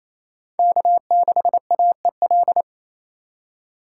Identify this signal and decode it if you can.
Morse code, heard as a keyed tone, about 27 wpm, 700 Hz. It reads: K6AEL
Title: USA Call Signs - 4x Character Spacing - 28wpm